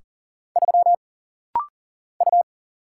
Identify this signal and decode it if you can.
Morse code, heard as a keyed tone, about 40 wpm, 700 Hz. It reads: 3 U
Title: Single Letter-Number Alt - Rapid-Fire 40wpm